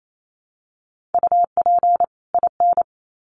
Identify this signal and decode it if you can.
Morse code, heard as a keyed tone, about 28 wpm, 700 Hz. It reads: UP IN
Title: Sets of 2 Words 28wpm